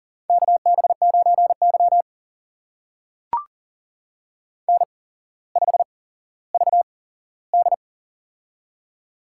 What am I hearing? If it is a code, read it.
Morse code, heard as a keyed tone, about 40 wpm, spaced as for 17 wpm, 700 Hz. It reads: KB9Y N5VD
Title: USA Call Signs - 4 Characters - 8x Character Spacing - 40wpm